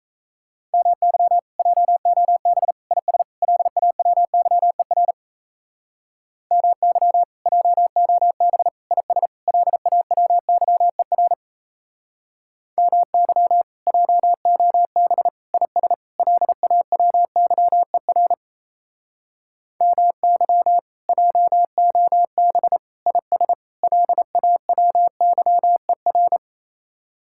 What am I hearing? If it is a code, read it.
Morse code, heard as a keyed tone, about 37 wpm, 700 Hz. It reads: MY JOB IS LAWYER MY JOB IS LAWYER MY JOB IS LAWYER MY JOB IS LAWYER